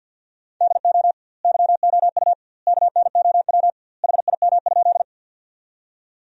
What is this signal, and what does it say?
Morse code, heard as a keyed tone, about 50 wpm, 700 Hz. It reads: DO YOU KNOW HIM?
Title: 2x Word-Spacing - Sentences from Top 100 Words - 50wpm